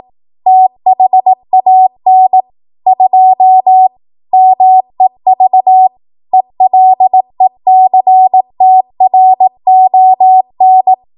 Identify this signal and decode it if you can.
Morse code, heard as a keyed tone, about 18 wpm, 750 Hz. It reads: THAN 2 MEV ELECTRON
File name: Morse / 18WPM